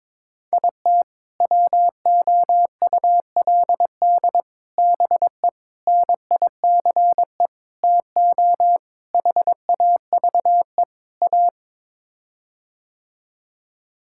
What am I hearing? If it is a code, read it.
Morse code, heard as a keyed tone, about 21 wpm, 700 Hz. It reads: IT WOULD BE NICE TO HAVE A
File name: practice / 22wpm / Sets of 7 Words - 22wpm